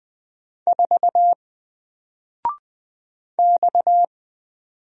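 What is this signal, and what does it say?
Morse code, heard as a keyed tone, about 20 wpm, 700 Hz. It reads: 4 X
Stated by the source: Single letter and number Morse drill at 20 wpm, from N4LCD.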